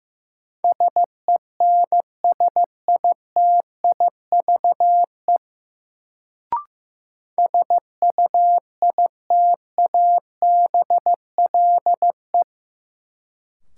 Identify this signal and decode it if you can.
Morse code, heard as a keyed tone, about 15 wpm, 700 Hz. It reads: SENSITIVE SUITABLE
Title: Sentences from Top 1400 Words - New Word Drill - 15wpm